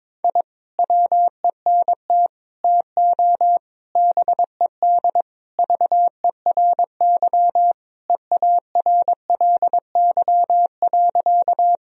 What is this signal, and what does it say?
Morse code, heard as a keyed tone, about 21 wpm, 700 Hz. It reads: I WENT TO BED VERY EARLY.
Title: Sentences from Top 1000 Words - Review Part 2 - 22wpm